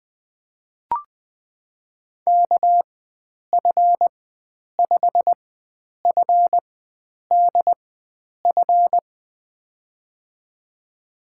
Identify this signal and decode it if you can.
Morse code, heard as a keyed tone, about 21 wpm, spaced as for 9 wpm, 700 Hz. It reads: KF5FDF
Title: USA Call Signs - 6 Characters - 4x Character Spacing - 20wpm